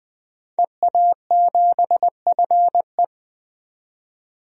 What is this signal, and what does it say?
Morse code, heard as a keyed tone, about 20 wpm, 700 Hz. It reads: EA7FE
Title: DXCC-100 Call Signs - 5 Characters - 20wpm